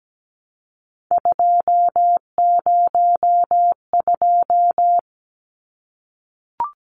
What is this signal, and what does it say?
Morse code, heard as a keyed tone, about 17 wpm, 700 Hz. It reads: 202
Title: Numbers - 3-Digit 17wpm